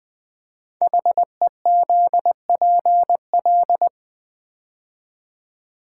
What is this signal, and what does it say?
Morse code, heard as a keyed tone, about 20 wpm, 700 Hz. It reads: HEZPL
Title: Code Groups - 5 Letters - 20wpm